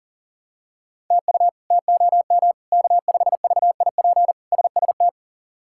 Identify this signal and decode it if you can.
Morse code, heard as a keyed tone, about 40 wpm, 700 Hz. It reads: TU TOM K5VIP SST
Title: K1USN SST QSO Practice - 40wpm